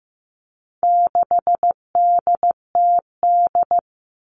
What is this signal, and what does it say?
Morse code, heard as a keyed tone, about 15 wpm, 700 Hz. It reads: 6DTD
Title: All Cut Numbers - 4-Digit 15wpm